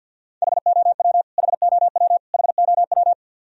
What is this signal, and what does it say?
Morse code, heard as a keyed tone, about 49 wpm, 700 Hz. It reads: HOW HOW HOW
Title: Top 100 Words Repeated 50wpm